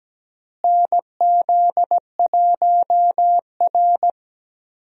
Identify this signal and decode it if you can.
Morse code, heard as a keyed tone, about 17 wpm, 700 Hz. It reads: NZ1R
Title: USA Call Signs - 4 Characters - Repeated 3x - 17wpm